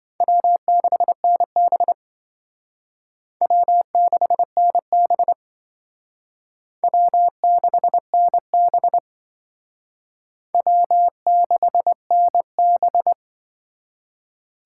Morse code, heard as a keyed tone, about 25 wpm, 700 Hz. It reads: W6NB W6NB W6NB W6NB
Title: Speed Racing - USA 1x2 Call Signs 20wpm